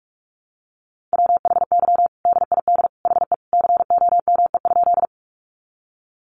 Morse code, heard as a keyed tone, about 44 wpm, 700 Hz. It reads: WHY DID HE COME?
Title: Sentences from Top 200 Words 45wpm